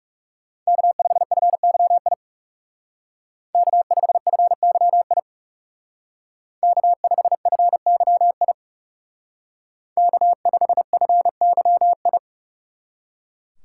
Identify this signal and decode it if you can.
Morse code, heard as a keyed tone, about 35 wpm, 700 Hz. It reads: K5FYI K5FYI K5FYI K5FYI
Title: Speed Racing - USA 1x3 Call Signs 30wpm